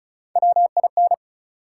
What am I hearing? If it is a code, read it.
Morse code, heard as a keyed tone, about 34 wpm, 700 Hz. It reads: WIN
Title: Three Letter Words - 35wpm